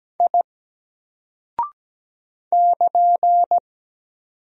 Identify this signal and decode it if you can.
Morse code, heard as a keyed tone, about 17 wpm, 700 Hz. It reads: I <KN>
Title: Single Character (Letter:Number:Punctuation:Prosign) Alt 17wpm